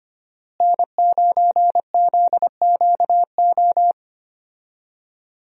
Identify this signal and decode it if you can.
Morse code, heard as a keyed tone, about 25 wpm, 700 Hz. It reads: N9ZQO
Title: USA Call Signs - 5 Characters - Repeated 2x - 25wpm